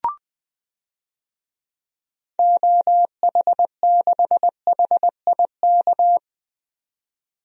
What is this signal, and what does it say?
Morse code, heard as a keyed tone, about 20 wpm, 700 Hz. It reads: OH6HIK
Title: DXCC-100 Call Signs - 6 Characters - 20wpm